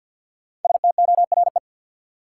Morse code, heard as a keyed tone, about 52 wpm, 700 Hz. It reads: STORE
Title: Sentences from Top 1000 Words - Review Part 5 - New Word Drill - 50wpm